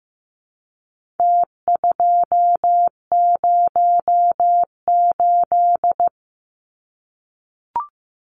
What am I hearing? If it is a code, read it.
Morse code, heard as a keyed tone, about 15 wpm, 700 Hz. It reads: T208
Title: Numbers - Contest Style - 4-Digit 15wpm